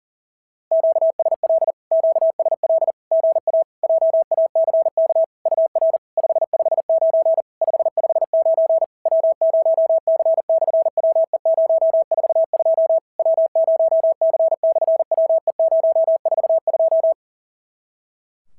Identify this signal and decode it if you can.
Morse code, heard as a keyed tone, about 40 wpm, 650 Hz. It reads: QSL QSL GA JACK UR 559 559 W0C/WE042 W0C/WE042
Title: SOTA QSO Practice 40wpm